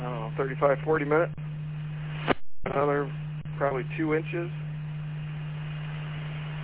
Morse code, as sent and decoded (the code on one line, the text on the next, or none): none